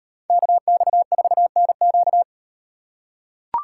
Morse code, heard as a keyed tone, about 38 wpm, 700 Hz. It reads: KX4NQ